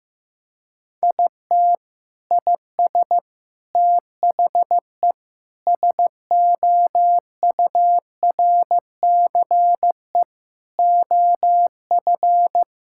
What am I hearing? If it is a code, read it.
Morse code, heard as a keyed tone, about 15 wpm, 700 Hz. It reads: IT IS THE SOURCE OF
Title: Sentences from Top 1000 Words - Review Part 5 - 15wpm